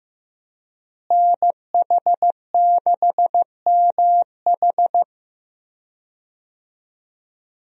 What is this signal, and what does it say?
Morse code, heard as a keyed tone, about 15 wpm, 700 Hz. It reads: NH6MH
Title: USA Call Signs - 5 Characters - Repeated 2x - 15wpm